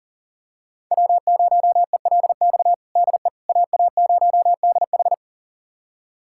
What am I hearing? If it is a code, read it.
Morse code, heard as a keyed tone, about 37 wpm, 700 Hz. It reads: W0ELX DE AA0DH
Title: USA Callsign DE Callsign - 5 Characters - 40wpm